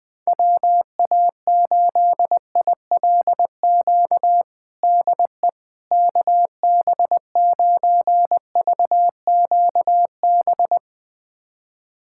Morse code, heard as a keyed tone, about 20 wpm, 700 Hz. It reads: WA8ILQ DE KB9VQB
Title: USA Callsign DE Callsign 20wpm